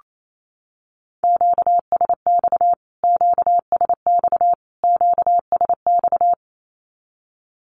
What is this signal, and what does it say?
Morse code, heard as a keyed tone, about 28 wpm, 700 Hz. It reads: QSX QSX QSX
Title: Q-Codes Repeated 3x 28wpm